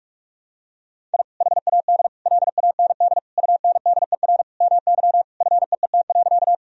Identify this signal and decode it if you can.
Morse code, heard as a keyed tone, about 47 wpm, 700 Hz. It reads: I HAD LAND UNDER MY FEET.
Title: Sentences from Top 1000 Words - Review Part 1 - 45wpm